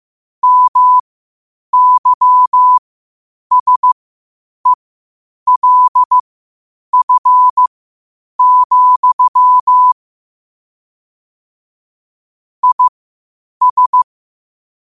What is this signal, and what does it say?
Morse code, heard as a keyed tone, about 15 wpm, spaced as for 8 wpm, 1000 Hz. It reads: MYSELF, IS
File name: qotdfiles / 8wpm